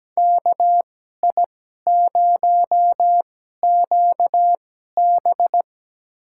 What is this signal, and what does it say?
Morse code, heard as a keyed tone, about 17 wpm, spaced as for 12 wpm, 700 Hz. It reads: KI0QB
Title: USA Call Signs - 5 Characters - 2x Character Spacing - 17wpm